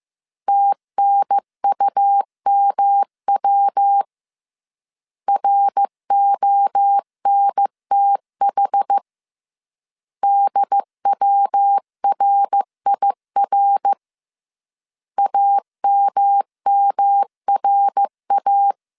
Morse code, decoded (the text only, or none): TNUMW RONTH DWRIR AMMRA